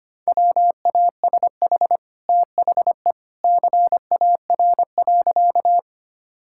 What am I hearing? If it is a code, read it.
Morse code, heard as a keyed tone, about 25 wpm, 690 Hz. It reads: WASH THE CAR.
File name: Easy Sentences with Hints - 25wpm